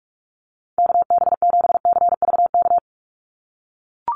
Morse code, heard as a keyed tone, about 44 wpm, 700 Hz. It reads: KB7CVK
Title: USA Call Signs 45wpm